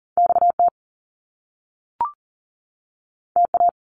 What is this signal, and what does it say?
Morse code, heard as a keyed tone, about 38 wpm, 700 Hz. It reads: XT TA
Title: Top Two Letter Combinations in Words 40wpm